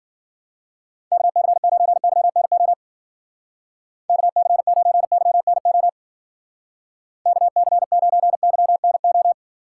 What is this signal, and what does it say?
Morse code, heard as a keyed tone, about 50 wpm, 700 Hz. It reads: KC9YNO KC9YNO KC9YNO
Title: Speed Racing - USA 2x3 Call Signs 40wpm